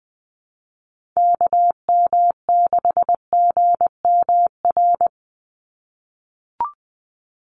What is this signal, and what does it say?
Morse code, heard as a keyed tone, about 20 wpm, 700 Hz. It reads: KM6GMR
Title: USA Call Signs 20wpm